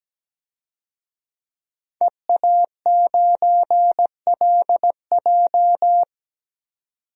Morse code, heard as a keyed tone, about 17 wpm, 700 Hz. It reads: EA9LJ
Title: DXCC-100 Call Signs - 5 Characters - 17wpm